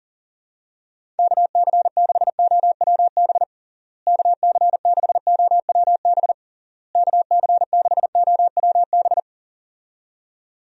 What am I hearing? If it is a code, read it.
Morse code, heard as a keyed tone, about 37 wpm, 700 Hz. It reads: KC6OWB KC6OWB KC6OWB
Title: USA Call Signs - 6 Characters - Repeated 3x - 40wpm